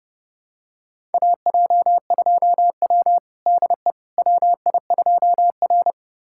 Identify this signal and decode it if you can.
Morse code, heard as a keyed tone, about 30 wpm, 700 Hz. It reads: AJ2W DE WI2R
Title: USA Callsign DE Callsign - 4 Characters - 30wpm